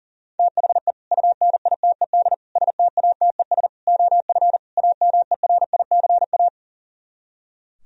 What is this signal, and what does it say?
Morse code, heard as a keyed tone, about 50 wpm, 700 Hz. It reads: THE UNITED STATES OF AMERICA